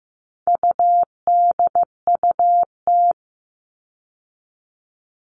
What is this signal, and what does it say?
Morse code, heard as a keyed tone, about 15 wpm, 700 Hz. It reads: UDUT
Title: All Cut Numbers - 4-Digit 15wpm